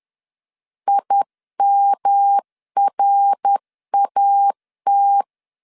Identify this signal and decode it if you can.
Morse code, heard as a keyed tone, about 10 wpm, 800 Hz. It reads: IMRAT
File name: 10wpm